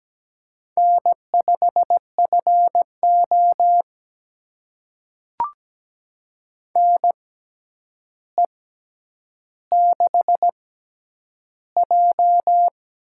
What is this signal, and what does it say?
Morse code, heard as a keyed tone, about 17 wpm, spaced as for 9 wpm, 700 Hz. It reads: N5FO NE6J
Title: USA Call Signs - 6x Character Spacing - 17wpm